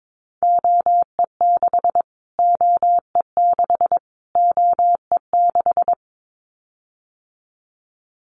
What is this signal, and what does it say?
Morse code, heard as a keyed tone, about 22 wpm, 700 Hz. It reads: OE6 OE6 OE6
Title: Callsign Prefixes for Top 25 DXCC - Repeated 3x - 22wpm